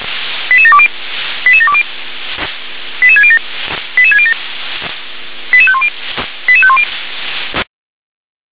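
Five-tone selective call (ZVEI), heard as quick runs of five tones, tones of about 70 ms. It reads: 88720 88720 88607 88607 88510 88510
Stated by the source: Five-tone ZVEI selective call received on an SDR.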